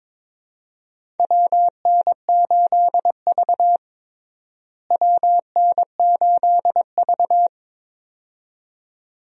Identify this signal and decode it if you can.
Morse code, heard as a keyed tone, about 21 wpm, 700 Hz. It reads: WN8V WN8V
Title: USA Call Signs - 4 Characters - Repeated 2x - 22wpm